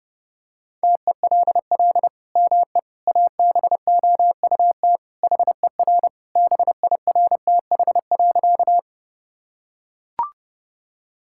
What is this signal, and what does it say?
Morse code, heard as a keyed tone, about 30 wpm, 700 Hz. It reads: TELL ME ABOUT HER BIRTH.
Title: Sentences from Top 1000 Words - Review Part 5 - 30wpm